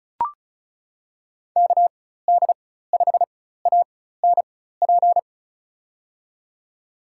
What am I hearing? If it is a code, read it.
Morse code, heard as a keyed tone, about 36 wpm, spaced as for 16 wpm, 700 Hz. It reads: KD5ANP
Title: USA Call Signs - 6 Characters - 4x Character Spacing - 35wpm